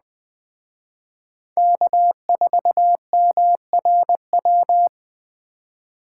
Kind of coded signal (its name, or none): Morse code